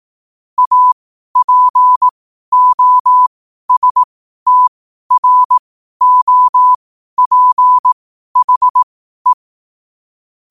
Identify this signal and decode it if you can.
Morse code, heard as a keyed tone, about 18 wpm, spaced as for 12 wpm, 1000 Hz. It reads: APOSTROPHE
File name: qotdfiles / 12wpm